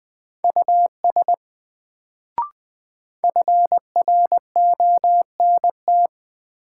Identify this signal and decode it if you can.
Morse code, heard as a keyed tone, about 20 wpm, 700 Hz. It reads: US FRONT
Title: Sets of 2 Words - New Word Drill - 20wpm